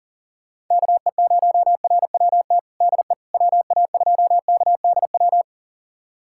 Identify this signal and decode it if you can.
Morse code, heard as a keyed tone, about 40 wpm, 700 Hz. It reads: KE0RWT DE WA2KDW
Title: USA Callsign DE Callsign - 6 Characters - 40wpm